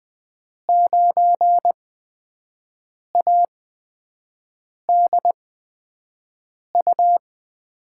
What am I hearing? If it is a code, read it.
Morse code, heard as a keyed tone, about 21 wpm, spaced as for 6 wpm, 700 Hz. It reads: 9ADU
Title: USA Call Signs - 8x Character Spacing - 20wpm